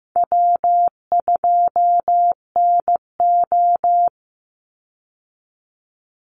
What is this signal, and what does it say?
Morse code, heard as a keyed tone, about 15 wpm, 700 Hz. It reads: W2NO
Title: Easy Call Signs Repeated 2x - 15wpm